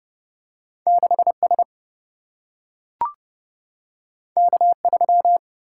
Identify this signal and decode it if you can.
Morse code, heard as a keyed tone, about 30 wpm, 700 Hz. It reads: 6 S K 3
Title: Single-Letter Single-Number 30wpm